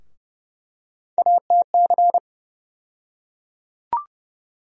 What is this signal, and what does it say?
Morse code, heard as a keyed tone, about 30 wpm, 700 Hz. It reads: ATC
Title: USA Call Sign Postfixes - 3 Letters 30wpm